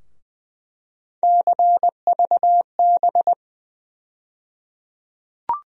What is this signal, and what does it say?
Morse code, heard as a keyed tone, about 20 wpm, 700 Hz. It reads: CVB
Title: USA Call Sign Postfixes - 3 Letters 20wpm